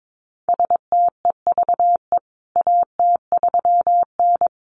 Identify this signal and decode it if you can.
Morse code, heard as a keyed tone, about 22 wpm, 700 Hz. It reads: STEVE AT3N
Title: CWT Exchange 22wpm